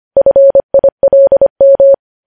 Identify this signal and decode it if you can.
Morse code, heard as a keyed tone, about 25 wpm, 550 Hz. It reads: FILM
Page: news headlines in morse code at 25 wpm